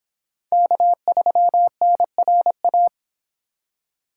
Morse code, heard as a keyed tone, about 31 wpm, 700 Hz. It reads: K3NRA